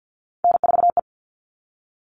Morse code, heard as a keyed tone, about 50 wpm, 700 Hz. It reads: N4E
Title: All Cut Numbers - 3-Digit 50wpm